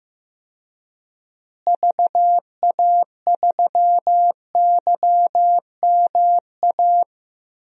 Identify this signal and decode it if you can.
Morse code, heard as a keyed tone, about 15 wpm, 700 Hz. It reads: VA3YMA
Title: DXCC-100 Call Signs - 6 Characters - Repeated 2x - 15wpm